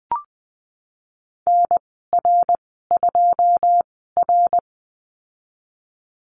Morse code, heard as a keyed tone, about 20 wpm, spaced as for 15 wpm, 700 Hz. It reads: NR2R
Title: 082 Lesson 14g - Number 2 - Call Signs 20wpm